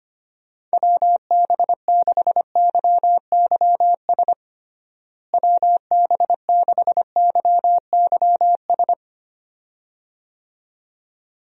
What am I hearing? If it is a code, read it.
Morse code, heard as a keyed tone, about 25 wpm, 700 Hz. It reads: WB6YYS WB6YYS
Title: USA Call Signs - 6 Characters - Repeated 2x - 25wpm